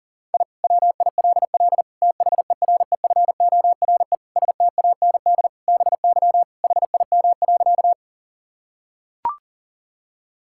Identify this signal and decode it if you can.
Morse code, heard as a keyed tone, about 40 wpm, 700 Hz. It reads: I WILL THEREFORE STAND BY HIM.
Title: Sentences from Top 1000 Words - Review Part 5 - 40wpm